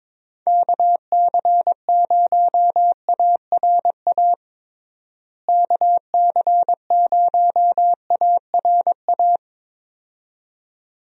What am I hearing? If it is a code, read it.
Morse code, heard as a keyed tone, about 22 wpm, 700 Hz. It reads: KC0ARA KC0ARA
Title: USA Call Signs - 6 Characters - Repeated 2x - 22wpm